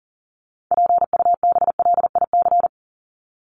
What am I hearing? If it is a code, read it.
Morse code, heard as a keyed tone, about 40 wpm, 700 Hz. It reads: PUBLIC
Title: Top 501-750 Words 40wpm